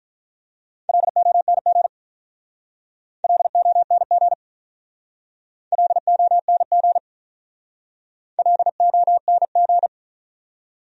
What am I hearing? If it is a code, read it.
Morse code, heard as a keyed tone, about 44 wpm, 700 Hz. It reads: LONG LONG LONG LONG